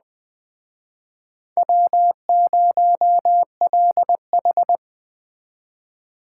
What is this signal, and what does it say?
Morse code, heard as a keyed tone, about 21 wpm, 700 Hz. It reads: W0LH